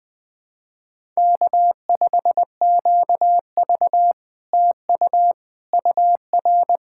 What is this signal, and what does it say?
Morse code, heard as a keyed tone, about 20 wpm, 700 Hz. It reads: K5QV TU UR
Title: SOTA QSO Practice 20wpm